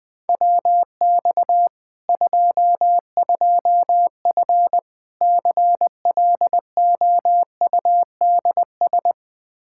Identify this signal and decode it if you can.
Morse code, heard as a keyed tone, about 20 wpm, 700 Hz. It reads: WX 22F CLOUDS
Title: Practice QSO - Exchange 2 - Info - 20wpm